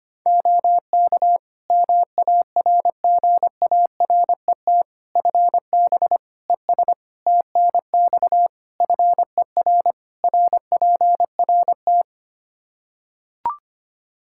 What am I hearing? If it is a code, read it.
Morse code, heard as a keyed tone, about 25 wpm, 700 Hz. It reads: OK MARGARET FB ES TNX FER RPRT
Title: Practice QSO - Exchange 2 - Info - 25wpm